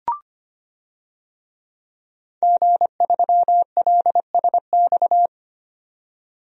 Morse code, heard as a keyed tone, about 25 wpm, 700 Hz. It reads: G3LSX
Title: DXCC-100 Call Signs - 5 Characters - 25wpm